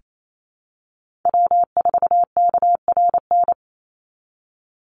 Morse code, heard as a keyed tone, about 28 wpm, 700 Hz. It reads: W4KRN
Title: NA SOTA Chaser Callsigns 28wpm